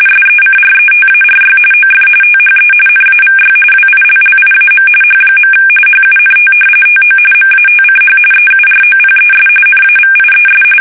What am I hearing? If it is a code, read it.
RTTY, about 45.45 baud. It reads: SECRETARY OF THE NAVY WILLIAM H. HUNT TO MATCH EUROPES NAVIES
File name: fsk850 / battleships